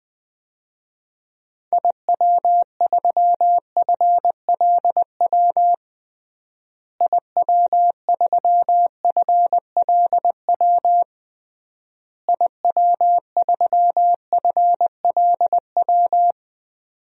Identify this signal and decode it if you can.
Morse code, heard as a keyed tone, about 21 wpm, 700 Hz. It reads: IW3FLW IW3FLW IW3FLW
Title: DXCC-100 Call Signs - 6 Characters - Repeated 3x - 20wpm